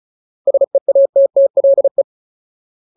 Morse code, heard as a keyed tone, about 34 wpm, 550 Hz. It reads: SEATTLE
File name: Ultimate Word List / 35WPM Ultimate